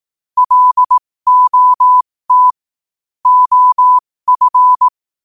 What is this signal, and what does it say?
Morse code, heard as a keyed tone, about 18 wpm, spaced as for 15 wpm, 1000 Hz. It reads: LOT OF
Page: New quotes every day in morse code at 15 Words per minute.